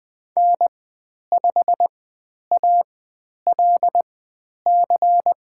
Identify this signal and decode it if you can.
Morse code, heard as a keyed tone, about 20 wpm, spaced as for 10 wpm, 700 Hz. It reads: N5ALC
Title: 095 Lesson 16g - Letter c - Call Signs 20wpm